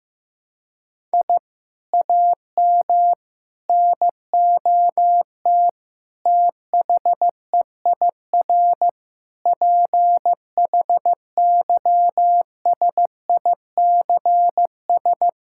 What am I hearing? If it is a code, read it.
Morse code, heard as a keyed tone, about 16 wpm, 700 Hz. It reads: I AM NOT THEIR PHYSICS
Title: Sentences from Top 1000 Words - Review Part 6 - 15wpm